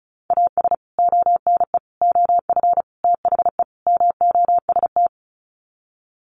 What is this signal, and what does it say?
Morse code, heard as a keyed tone, about 34 wpm, 700 Hz. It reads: AS ONE OF THE MOST
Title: Sets of 5 Words 35wpm